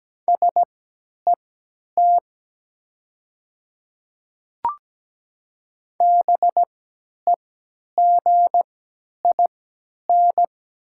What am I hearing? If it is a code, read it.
Morse code, heard as a keyed tone, about 17 wpm, spaced as for 10 wpm, 690 Hz. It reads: SET BEGIN
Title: 3x Character Spacing - Top 300 Words - 17wpm